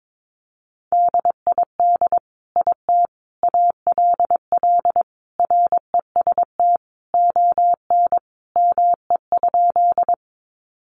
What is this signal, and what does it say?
Morse code, heard as a keyed tone, about 22 wpm, 700 Hz. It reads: DID IT ALL REST ON ME?
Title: Sentences from Top 500 Words 22wpm